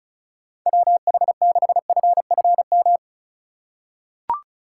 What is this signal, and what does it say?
Morse code, heard as a keyed tone, about 36 wpm, 700 Hz. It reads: WH6FFM
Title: USA Call Signs - 6 Characters - 2x Character Spacing - 35wpm